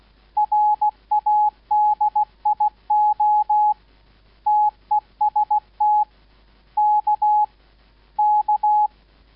Morse code, heard as a keyed tone, about 16 wpm, 840 Hz. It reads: RADIO TEST K K